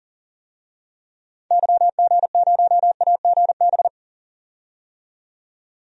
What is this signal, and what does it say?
Morse code, heard as a keyed tone, about 40 wpm, 700 Hz. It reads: YG0AGB
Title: DXCC-100 Call Signs - 6 Characters - 40wpm